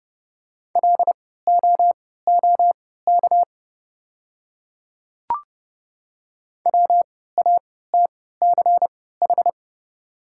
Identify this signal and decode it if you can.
Morse code, heard as a keyed tone, about 30 wpm, spaced as for 17 wpm, 700 Hz. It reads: LOOK WATCH
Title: 3x Character Spacing - Top 300 Words - 30wpm